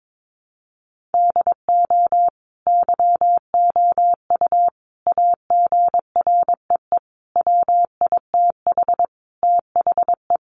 Morse code, heard as a keyed tone, about 22 wpm, 700 Hz. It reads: DO YOU AGREE WITH THE
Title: Sentences from Top 1400 Words 22wpm